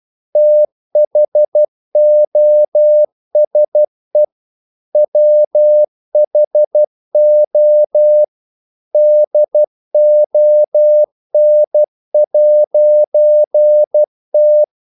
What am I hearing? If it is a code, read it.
Morse code, heard as a keyed tone, about 12 wpm, 600 Hz. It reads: THOSE WHO DON'T
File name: CW / Practice / EnglishQuotations0112